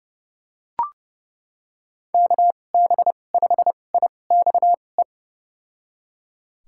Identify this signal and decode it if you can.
Morse code, heard as a keyed tone, about 32 wpm, spaced as for 22 wpm, 700 Hz. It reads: KB5IXE
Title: USA Call Signs - 6 Characters - 2x Character Spacing - 30wpm